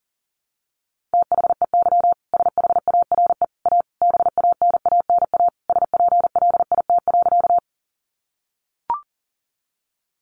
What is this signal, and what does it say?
Morse code, heard as a keyed tone, about 40 wpm, 700 Hz. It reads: THEY SHARE A BANANA SPLIT.
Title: Easy Phrases 40wpm